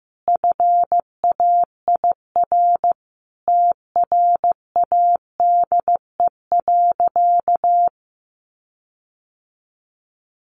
Morse code, heard as a keyed tone, about 15 wpm, 700 Hz. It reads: FAIR TRADE.
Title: Sentences from Top 1300 Words 15wpm